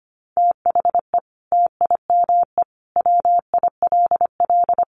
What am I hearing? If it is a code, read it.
Morse code, heard as a keyed tone, about 25 wpm, 700 Hz. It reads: THE TIME WILL
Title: Sentences from Top 100 Words 25wpm